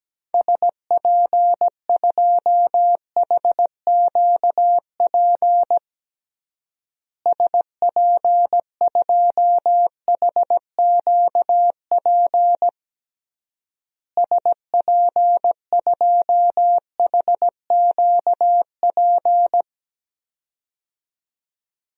Morse code, heard as a keyed tone, about 17 wpm, 700 Hz. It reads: SP2HQP SP2HQP SP2HQP
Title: DXCC-100 Call Signs - 6 Characters - Repeated 3x - 17wpm